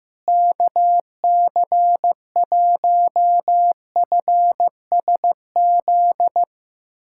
Morse code, heard as a keyed tone, about 15 wpm, 700 Hz. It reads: KC1FSZ
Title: K1USN SST QSO Practice - 15wpm